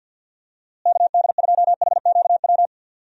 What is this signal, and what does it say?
Morse code, heard as a keyed tone, about 52 wpm, 700 Hz. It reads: KD2HQW
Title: USA Call Signs - 6 Characters - Repeated 2x - 50wpm